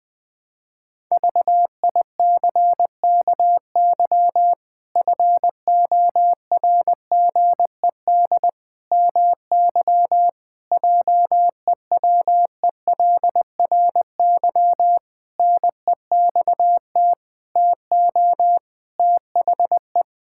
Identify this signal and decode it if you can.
Morse code, heard as a keyed tone, about 21 wpm, 700 Hz. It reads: VICKY FORGED MY JEWELRY NEXT TO THE
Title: Pangram Challenge v2 - 20wpm